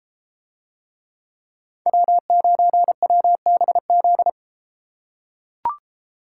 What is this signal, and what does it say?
Morse code, heard as a keyed tone, about 31 wpm, 700 Hz. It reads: W9WBZ